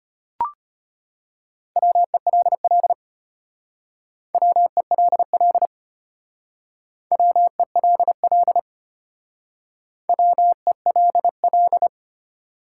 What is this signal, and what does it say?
Morse code, heard as a keyed tone, about 33 wpm, 700 Hz. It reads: WELL WELL WELL WELL